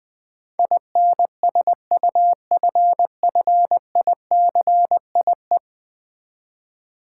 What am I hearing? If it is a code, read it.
Morse code, heard as a keyed tone, about 21 wpm, 700 Hz. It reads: INSUFFICIE
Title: Build A Word - 5 Letters At A Time - 12-Letter Words - 20wpm